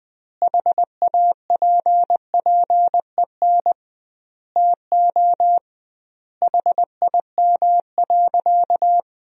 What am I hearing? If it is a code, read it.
Morse code, heard as a keyed tone, about 20 wpm, 700 Hz. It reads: HAPPEN TO HIM.
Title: 2x Word-Spacing - Sentences from Top 400 Words - 20wpm